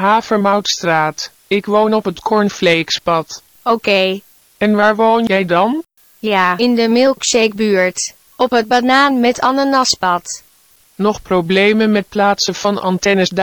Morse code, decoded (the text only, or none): none